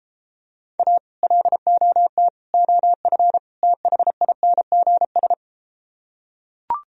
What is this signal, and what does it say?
Morse code, heard as a keyed tone, about 33 wpm, 700 Hz. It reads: A LOT OF THINGS